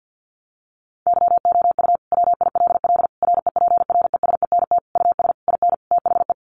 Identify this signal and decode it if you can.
Morse code, heard as a keyed tone, about 50 wpm, 700 Hz. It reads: YOU WILL REPRESENT US IN THE
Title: Sentences from Top 1200 Words 50wpm